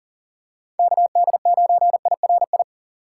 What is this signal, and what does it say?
Morse code, heard as a keyed tone, about 40 wpm, 700 Hz. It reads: KD9IRI
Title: USA Call Signs - 6x Character Spacing - 40wpm